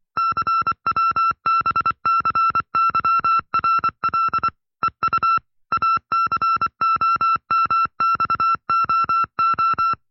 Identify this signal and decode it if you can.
Morse code, heard as a keyed tone, about 24 wpm, 1340 Hz. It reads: CWBCYRL EU ACOMXOO
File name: FreeFiles / CW-practice / 20WPM-side1